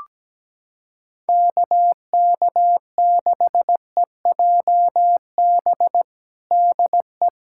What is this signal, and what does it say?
Morse code, heard as a keyed tone, about 17 wpm, 700 Hz. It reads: KK6EJB DE
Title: USA Callsign DE Callsign - 6 Characters - 17wpm